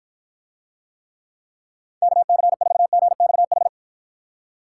Morse code, heard as a keyed tone, about 53 wpm, 700 Hz. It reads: KC4GXH